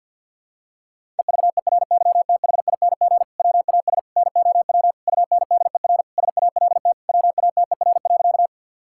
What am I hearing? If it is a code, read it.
Morse code, heard as a keyed tone, about 48 wpm, 700 Hz. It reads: EVERYTHING WAS NOW UNDER SALT WATER.
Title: Sentences from Top 1000 Words - Review Part 5 - 50wpm